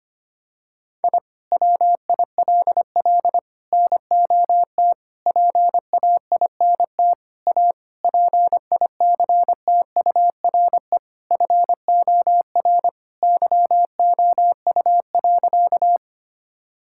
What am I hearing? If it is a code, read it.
Morse code, heard as a keyed tone, about 25 wpm, 700 Hz. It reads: I WILL NOT PAINT A PICTURE FOR YOU.
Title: Sentences from Top 1000 Words - Review Part 6 - 25wpm